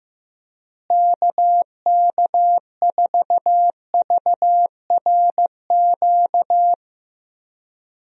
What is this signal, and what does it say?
Morse code, heard as a keyed tone, about 15 wpm, 700 Hz. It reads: KK4VRQ
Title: USA Call Signs - 6 Characters - Repeated 3x - 15wpm